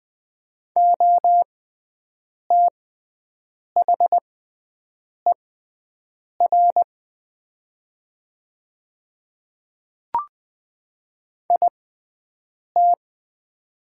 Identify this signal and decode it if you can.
Morse code, heard as a keyed tone, about 20 wpm, spaced as for 7 wpm, 700 Hz. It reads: OTHER IT
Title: 6x Character Spacing - Top 200 Words - 20wpm